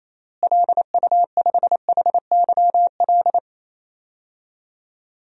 Morse code, heard as a keyed tone, about 28 wpm, 700 Hz. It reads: LU5HYL
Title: DXCC-100 Call Signs - 6 Characters - Repeated 2x - 28wpm